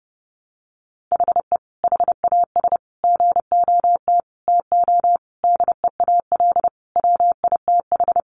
Morse code, heard as a keyed tone, about 30 wpm, 700 Hz. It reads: HE HAS GOT TO DEAL WITH
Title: Sentences from Top 800 Words 30wpm